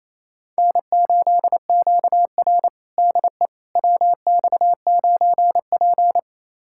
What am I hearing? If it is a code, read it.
Morse code, heard as a keyed tone, about 29 wpm, 690 Hz. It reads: N8QR DE WX9P
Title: USA Callsign DE Callsign - 4 Characters - 28wpm